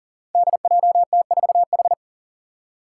Morse code, heard as a keyed tone, about 40 wpm, 700 Hz. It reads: DJT4H
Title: Five Challenging Letter-Number Combinations 40wpm